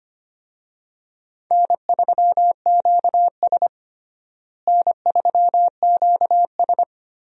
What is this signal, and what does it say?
Morse code, heard as a keyed tone, about 25 wpm, 700 Hz. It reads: N3QS N3QS
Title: USA Call Signs - 4 Characters - Repeated 3x - 25wpm